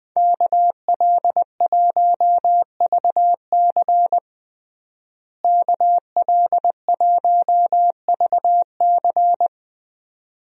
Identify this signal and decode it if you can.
Morse code, heard as a keyed tone, about 20 wpm, 700 Hz. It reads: KL1VC KL1VC
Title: USA Call Signs - 5 Characters - Repeated 2x - 20wpm